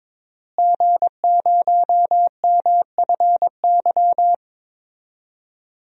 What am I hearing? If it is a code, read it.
Morse code, heard as a keyed tone, about 21 wpm, 700 Hz. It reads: G0MFY
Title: DXCC-100 Call Signs - 5 Characters - Repeated 3x - 22wpm